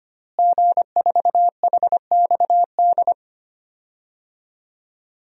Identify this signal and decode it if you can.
Morse code, heard as a keyed tone, about 25 wpm, 700 Hz. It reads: G4HXD
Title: DXCC-100 Call Signs - 5 Characters - Repeated 2x - 25wpm